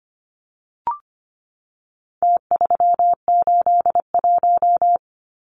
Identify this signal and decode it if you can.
Morse code, heard as a keyed tone, about 25 wpm, 700 Hz. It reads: T381
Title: Numbers - Contest Style - 4-Digit 25wpm